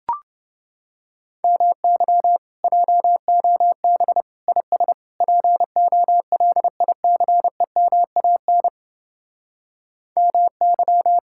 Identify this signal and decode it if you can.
Morse code, heard as a keyed tone, about 32 wpm, 700 Hz. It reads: MY JOB IS POLICEMAN MY